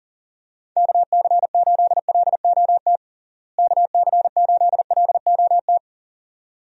Morse code, heard as a keyed tone, about 40 wpm, 700 Hz. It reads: KC8LOT KC8LOT
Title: USA Call Signs - 6 Characters - Repeated 2x - 40wpm